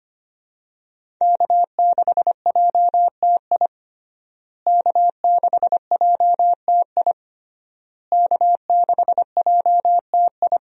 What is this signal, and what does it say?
Morse code, heard as a keyed tone, about 25 wpm, 700 Hz. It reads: K6JTI K6JTI K6JTI
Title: USA Call Signs - 5 Characters - Repeated 3x - 25wpm